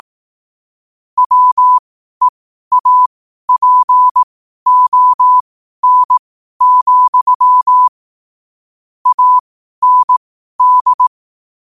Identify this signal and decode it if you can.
Morse code, heard as a keyed tone, about 18 wpm, spaced as for 12 wpm, 1000 Hz. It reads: WEAPON, AND
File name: qotdfiles / 12wpm